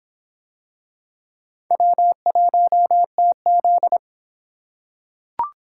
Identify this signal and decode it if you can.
Morse code, heard as a keyed tone, about 23 wpm, 700 Hz. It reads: W1TZ